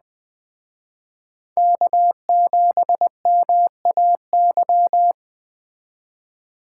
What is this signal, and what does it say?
Morse code, heard as a keyed tone, about 20 wpm, 700 Hz. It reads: K7MAY
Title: Speed Racing - USA Call Signs 17wpm